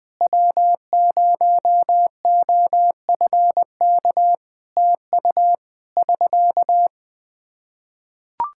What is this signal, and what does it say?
Morse code, heard as a keyed tone, about 21 wpm, 700 Hz. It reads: W0OFK TU <SK>
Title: K1USN SST QSO Practice 20wpm